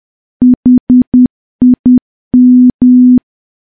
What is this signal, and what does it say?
Morse code, heard as a keyed tone, about 10 wpm, 250 Hz. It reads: HIM
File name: morse / morse-training-10WPM